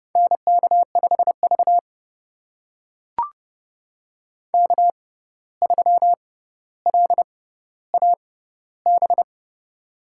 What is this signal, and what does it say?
Morse code, heard as a keyed tone, about 30 wpm, spaced as for 14 wpm, 700 Hz. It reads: NK5V K3LAB
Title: USA Call Signs - 6x Character Spacing - 30wpm